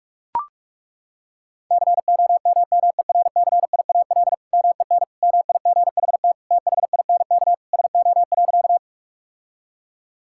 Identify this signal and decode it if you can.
Morse code, heard as a keyed tone, about 41 wpm, 690 Hz. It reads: COMMERCIAL MEN MIGHT THINK SO.
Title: Sentences from Top 1000 Words - Review Part 5 - 45wpm